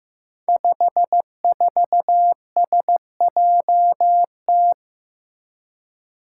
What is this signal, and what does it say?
Morse code, heard as a keyed tone, about 15 wpm, 700 Hz. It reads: 54SJT
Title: Five Challenging Letter-Number Combinations 15wpm